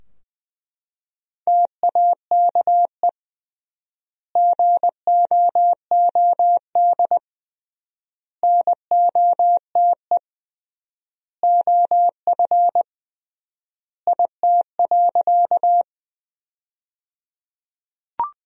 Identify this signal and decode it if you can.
Morse code, heard as a keyed tone, about 20 wpm, 700 Hz. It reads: TAKE GOOD NOTE OF IT.
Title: 3x Word-Spacing - Sentences from Top 500 Words - 20wpm